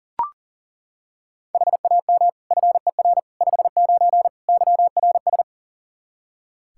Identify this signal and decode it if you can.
Morse code, heard as a keyed tone, about 37 wpm, 700 Hz. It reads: HAM FER 59 YRS
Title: Practice QSO - Exchange 3 - More Info - 40wpm